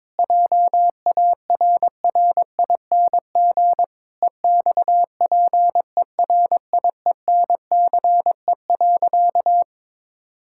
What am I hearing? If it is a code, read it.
Morse code, heard as a keyed tone, about 22 wpm, 700 Hz. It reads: JARRING EXPERIENCE.
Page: morse-code-ninja-practice-22wpm